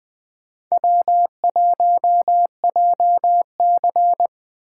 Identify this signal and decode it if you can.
Morse code, heard as a keyed tone, about 20 wpm, 700 Hz. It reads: W1JC
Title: Speed Racing - USA 1x2 Call Signs 17wpm